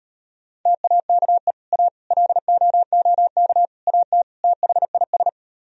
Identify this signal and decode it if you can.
Morse code, heard as a keyed tone, about 33 wpm, 700 Hz. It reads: TAKE A LOOK AT THIS